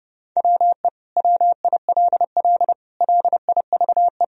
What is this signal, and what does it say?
Morse code, heard as a keyed tone, about 30 wpm, 700 Hz. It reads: WE WILL LIVE
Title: Sentences from Top 1000 Words - Review Part 4 - 30wpm